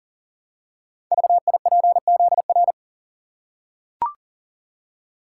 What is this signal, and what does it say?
Morse code, heard as a keyed tone, about 44 wpm, 700 Hz. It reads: VIPZR
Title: Code Groups - 5 Letters - 40wpm